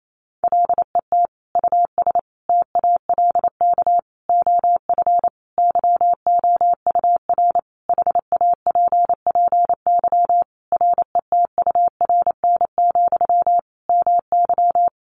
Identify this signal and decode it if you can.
Morse code, heard as a keyed tone, about 28 wpm, 700 Hz. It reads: LET US TALK OF YOUR HAPPY RETURN, MY
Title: Sentences from Top 1000 Words 28wpm